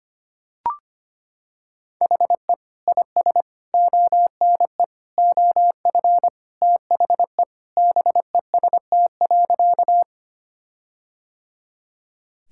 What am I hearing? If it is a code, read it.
Morse code, heard as a keyed tone, about 25 wpm, 700 Hz. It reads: HE IS ONE OF THE BEST.
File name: practice / 25wpm / Sentences with 4-Word Phrases - 25wpm